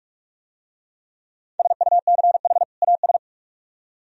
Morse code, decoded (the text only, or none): SUCH AS